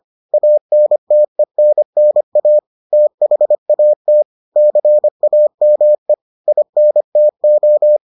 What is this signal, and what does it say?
Morse code, heard as a keyed tone, about 25 wpm, 600 Hz. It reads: ANTENNA THAT CAME INTO